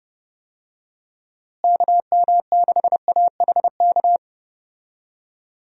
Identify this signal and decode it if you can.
Morse code, heard as a keyed tone, about 30 wpm, 700 Hz. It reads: KM6AHK
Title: USA Call Signs - 6 Characters - Repeated 3x - 30wpm